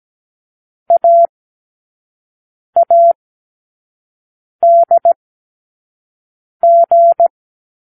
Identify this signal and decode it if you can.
Morse code, heard as a keyed tone, about 17 wpm, spaced as for 5 wpm, 680 Hz. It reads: AADG
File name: k7qo / Lesson15(A-J)